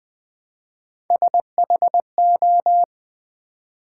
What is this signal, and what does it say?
Morse code, heard as a keyed tone, about 20 wpm, 700 Hz. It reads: SHO
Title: Build A Word - 1 Letter At A Time - 3 to 5-Letter Words - 20wpm